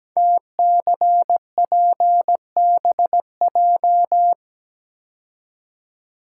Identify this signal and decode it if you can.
Morse code, heard as a keyed tone, about 17 wpm, 700 Hz. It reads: TCPBJ
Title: Code Groups - 5 Letters - 17wpm